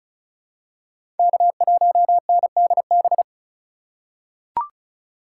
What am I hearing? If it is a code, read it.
Morse code, heard as a keyed tone, about 34 wpm, 700 Hz. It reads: K1NDB
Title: USA Call Signs - 5 Characters - 4x Character Spacing - 35wpm